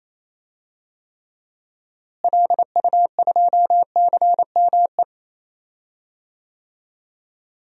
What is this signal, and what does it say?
Morse code, heard as a keyed tone, about 28 wpm, 700 Hz. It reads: LU2CME
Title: DXCC-100 Call Signs - 6 Characters - Repeated 3x - 28wpm